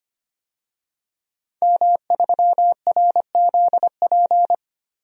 Morse code, heard as a keyed tone, about 25 wpm, 700 Hz. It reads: M3RZP
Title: DXCC-100 Call Signs - 5 Characters - Repeated 2x - 25wpm